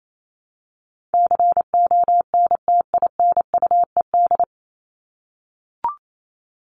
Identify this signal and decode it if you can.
Morse code, heard as a keyed tone, about 28 wpm, 700 Hz. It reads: CONTINUED
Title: ED Suffix - Top 100 Words with Base Words 28wpm